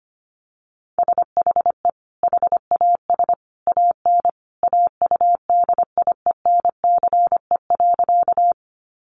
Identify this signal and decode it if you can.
Morse code, heard as a keyed tone, about 25 wpm, 700 Hz. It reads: SHE HAS AN AUDIENCE.
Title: Sentences from Top 800 Words 25wpm